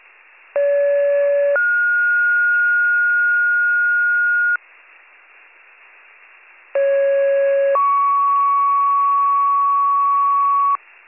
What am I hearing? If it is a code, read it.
Two-tone page, heard as a one-second tone, then a three-second tone, about 570 then 1430 Hz, about 550 then 1120 Hz. It is heard from 0.6 to 10.8 s.